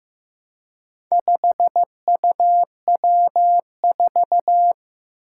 Morse code, heard as a keyed tone, about 15 wpm, 700 Hz. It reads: 5UW4
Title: Four Challenging Letter-Number Combinations 15wpm